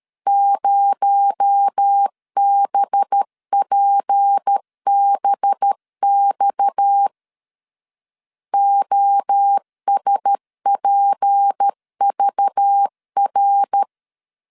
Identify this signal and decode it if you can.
Morse code, heard as a keyed tone, about 13 wpm, 800 Hz. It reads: 0BPBX OSPVR
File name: LEZIONI / 8 YJ45 / 8 YJ45 12wpm 2151